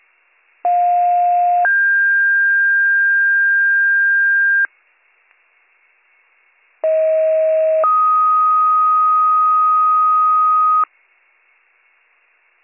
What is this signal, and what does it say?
Two-tone page, heard as a one-second tone, then a three-second tone, about 710 then 1600 Hz, about 640 then 1250 Hz. It is heard from 0.6 to 10.8 s.